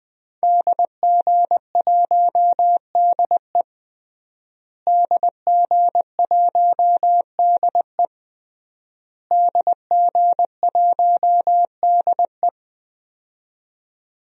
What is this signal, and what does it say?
Morse code, heard as a keyed tone, about 21 wpm, 700 Hz. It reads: DG1DE DG1DE DG1DE
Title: DXCC-100 Call Signs - 5 Characters - Repeated 3x - 20wpm